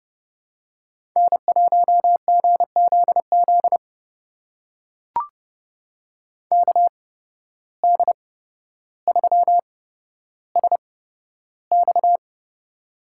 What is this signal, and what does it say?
Morse code, heard as a keyed tone, about 29 wpm, spaced as for 13 wpm, 700 Hz. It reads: N1GZZ KD3SX
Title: USA Call Signs - 5 Characters - 8x Character Spacing - 30wpm